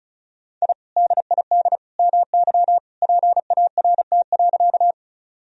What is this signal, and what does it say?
Morse code, heard as a keyed tone, about 36 wpm, 700 Hz. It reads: I DID MY PART.
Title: Sentences from Top 1000 Words - Review Part 1 - 35wpm